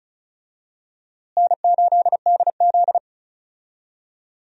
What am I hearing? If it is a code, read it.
Morse code, heard as a keyed tone, about 36 wpm, 700 Hz. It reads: N8DZ
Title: USA Call Signs - 4 Characters - Repeated 2x - 35wpm